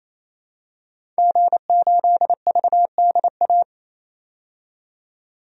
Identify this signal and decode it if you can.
Morse code, heard as a keyed tone, about 28 wpm, 700 Hz. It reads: G8VDA
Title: DXCC-100 Call Signs - 5 Characters - 28wpm